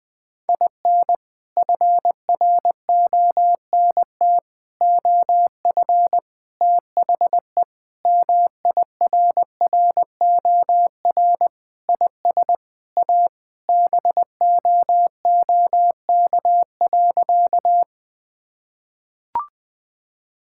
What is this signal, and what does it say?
Morse code, heard as a keyed tone, about 21 wpm, 700 Hz. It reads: IN FRONT OF THE MIRROR IS A BOOK.
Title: Sentences with 4-Word Phrases - Spoken Beginning - 20wpm